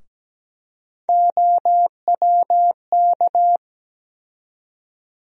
Morse code, heard as a keyed tone, about 17 wpm, 700 Hz. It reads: OWK
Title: USA Call Sign Postfixes - 3 Letters 17wpm